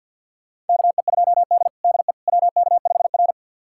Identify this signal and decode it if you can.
Morse code, heard as a keyed tone, about 53 wpm, 700 Hz. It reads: KE2D DE WK5R
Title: USA Callsign DE Callsign - 4 Characters - 50wpm